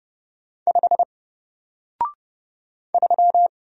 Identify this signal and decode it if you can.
Morse code, heard as a keyed tone, about 29 wpm, 700 Hz. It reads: 5 3
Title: Single Letter-Number Alt 30wpm